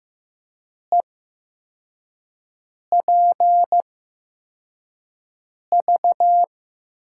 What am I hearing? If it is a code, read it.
Morse code, heard as a keyed tone, about 15 wpm, spaced as for 4 wpm, 700 Hz. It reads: EPV
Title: USA Call Signs - 5 Characters - 8x Character Spacing - 15wpm